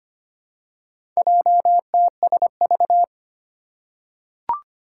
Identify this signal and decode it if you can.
Morse code, heard as a keyed tone, about 25 wpm, 700 Hz. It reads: JTSV
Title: Four Challenging Letter-Number Combinations 25wpm